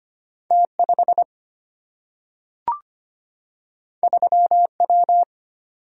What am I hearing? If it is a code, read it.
Morse code, heard as a keyed tone, about 25 wpm, 700 Hz. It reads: T5 3W
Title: Single-Letter Single-Number 25wpm